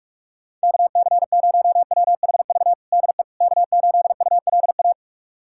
Morse code, heard as a keyed tone, about 46 wpm, 700 Hz. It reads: KC0WHV DE K8ULA
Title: USA Callsign DE Callsign 45wpm